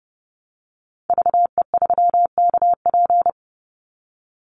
Morse code, heard as a keyed tone, about 28 wpm, 700 Hz. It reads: VE3KP
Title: Call Signs - 30wpm